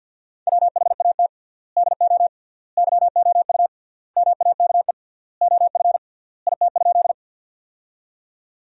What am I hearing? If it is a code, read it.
Morse code, heard as a keyed tone, about 50 wpm, 700 Hz. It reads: WHAT DO YOU MAKE OF IT?
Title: 3x Word-Spacing - Sentences from Top 100 Words - 50wpm